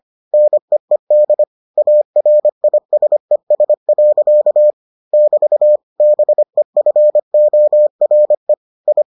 Morse code, heard as a keyed tone, about 25 wpm, 600 Hz. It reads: NEED ARISES. <BT> BEFORE I